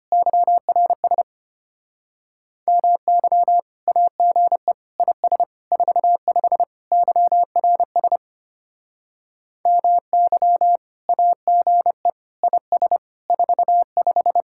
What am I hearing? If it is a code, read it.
Morse code, heard as a keyed tone, about 32 wpm, 700 Hz. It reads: YRS MY AGE IS 45 YRS MY AGE IS 45